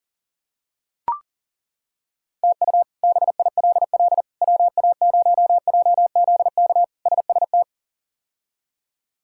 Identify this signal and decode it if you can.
Morse code, heard as a keyed tone, about 38 wpm, 700 Hz. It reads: TU BILL WA0JZK SST
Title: K1USN SST QSO Practice - 40wpm